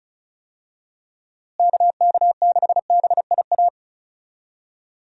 Morse code, heard as a keyed tone, about 36 wpm, 700 Hz. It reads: KK6BIA
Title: USA Call Signs - 6 Characters - Repeated 3x - 35wpm